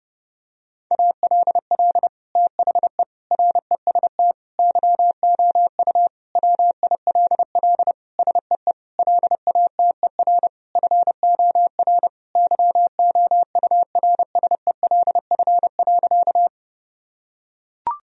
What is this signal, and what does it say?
Morse code, heard as a keyed tone, about 30 wpm, 700 Hz. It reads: ALL THE REST YOU WILL SEE LATER FOR YOURSELF.
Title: Sentences from Top 1000 Words - Review Part 4 - 30wpm